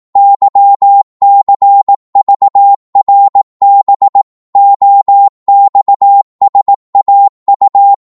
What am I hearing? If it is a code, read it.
Morse code, heard as a keyed tone, about 18 wpm, 800 Hz. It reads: YCVRB OXSAU
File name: Morse / ltr-18wpm